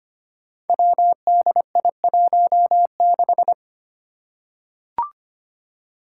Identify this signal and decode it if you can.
Morse code, heard as a keyed tone, about 25 wpm, 700 Hz. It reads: WDI16
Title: Five Challenging Letter-Number Combinations 25wpm